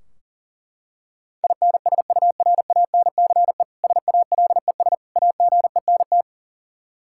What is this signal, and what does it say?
Morse code, heard as a keyed tone, about 40 wpm, 700 Hz. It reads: INSURANCE SALES AGENT
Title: Practice QSO - Exchange 3 - More Info - 40wpm